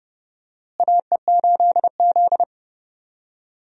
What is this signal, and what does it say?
Morse code, heard as a keyed tone, about 31 wpm, 700 Hz. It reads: AE8Z
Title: USA Call Signs - 4 Characters - Repeated 3x - 30wpm